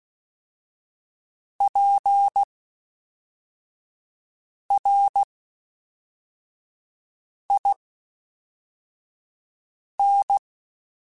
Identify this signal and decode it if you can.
Morse code, heard as a keyed tone, about 16 wpm, spaced as for 4 wpm, 800 Hz. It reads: PRIN
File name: Special INGs / Special INGs-mp3s / Fast ing @10